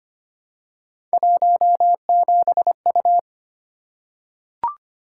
Morse code, heard as a keyed tone, about 25 wpm, 700 Hz. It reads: 17U
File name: Three Letter-Number Combinations - 25wpm